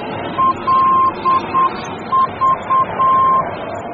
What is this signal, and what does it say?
Morse code, heard as a keyed tone, about 8 wpm, 1050 Hz, with noise. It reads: LV
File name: germany / mx / LV-D